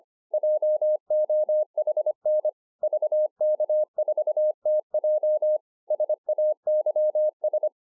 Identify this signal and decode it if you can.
Morse code, heard as a keyed tone, about 24 wpm, 610 Hz. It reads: JOHN VK4TJ SAYS
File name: 2011-04-mp3 / BEGINNERS DIARY 25WPM